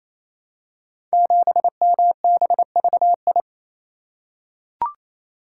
Morse code, heard as a keyed tone, about 29 wpm, 700 Hz. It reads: 7MBVI
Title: Code Groups - 5 Letter:Numbers - 28wpm